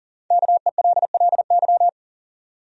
Morse code, heard as a keyed tone, about 39 wpm, 700 Hz. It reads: KELLY
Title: USA Common Names - Max 5 Letters - 40wpm